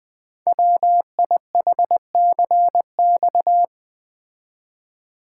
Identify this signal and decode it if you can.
Morse code, heard as a keyed tone, about 20 wpm, 700 Hz. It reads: WIHCX
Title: Code Groups - 5 Letters - 20wpm